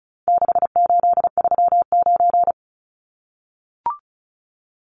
Morse code, heard as a keyed tone, about 34 wpm, 700 Hz. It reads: 6839
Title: Numbers - 4-Digit 35wpm